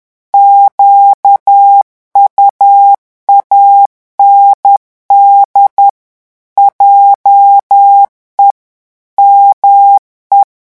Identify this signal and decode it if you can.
Morse code, heard as a keyed tone, about 10 wpm, 800 Hz. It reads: QUAND JE ME
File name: morse mp3 / revision francais 10 mots minute